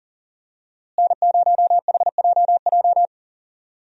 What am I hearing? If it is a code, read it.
Morse code, heard as a keyed tone, about 39 wpm, 700 Hz. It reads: N0HJJ
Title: USA Call Signs - 5 Characters - Repeated 2x - 40wpm